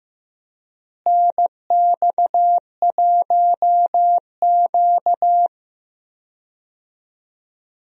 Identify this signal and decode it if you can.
Morse code, heard as a keyed tone, about 15 wpm, 700 Hz. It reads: NX1Q
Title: USA Call Signs - 4 Characters - Repeated 3x - 15wpm